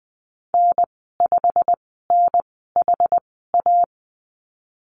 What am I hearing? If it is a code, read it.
Morse code, heard as a keyed tone, about 20 wpm, spaced as for 15 wpm, 700 Hz. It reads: N5NHA
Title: 167 Lesson 27g - Number 7 - Call Signs 20wpm